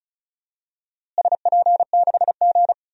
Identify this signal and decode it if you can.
Morse code, heard as a keyed tone, about 36 wpm, 700 Hz. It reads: SP6G
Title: DXCC-100 Call Signs - 4 Characters - Repeated 3x - 35wpm